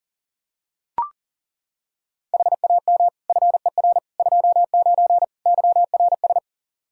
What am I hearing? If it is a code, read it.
Morse code, heard as a keyed tone, about 37 wpm, 700 Hz. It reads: HAM FER 29 YRS
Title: Practice QSO - Exchange 3 - More Info - 40wpm